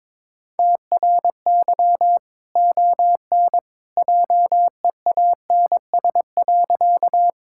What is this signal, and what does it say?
Morse code, heard as a keyed tone, about 22 wpm, 700 Hz. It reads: TRY ON JEANS.
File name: practice / 22wpm / Easy Sentences with Hints - 22wpm